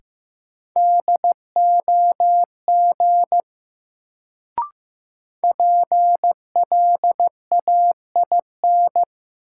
Morse code, heard as a keyed tone, about 15 wpm, 700 Hz. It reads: DOG PLAIN
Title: Sentences from Top 400 Words - New Word Drill - 15wpm